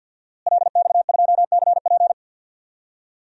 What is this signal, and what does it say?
Morse code, heard as a keyed tone, about 50 wpm, 700 Hz. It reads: LX2CP
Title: DXCC-100 Call Signs - 5 Characters - Repeated 2x - 50wpm